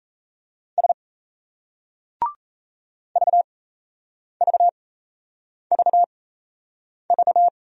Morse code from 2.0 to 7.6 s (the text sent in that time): VVVV